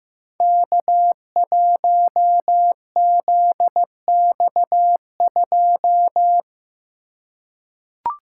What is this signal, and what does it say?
Morse code, heard as a keyed tone, about 15 wpm, 700 Hz. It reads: K1ZX2
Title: Code Groups - 5 Letter:Numbers - 15wpm